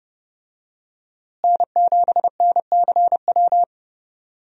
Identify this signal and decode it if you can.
Morse code, heard as a keyed tone, about 30 wpm, 700 Hz. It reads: N7NCW
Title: USA Call Signs - 5 Characters - Repeated 2x - 30wpm